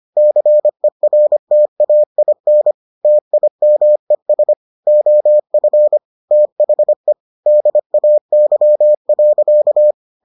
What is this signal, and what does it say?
Morse code, heard as a keyed tone, about 25 wpm, 600 Hz. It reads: CERTAIN TIMES OF THE DAY.